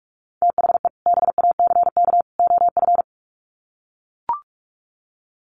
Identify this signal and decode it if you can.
Morse code, heard as a keyed tone, about 44 wpm, 700 Hz. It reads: THE BACK OF
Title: Sets of 3 Words 45wpm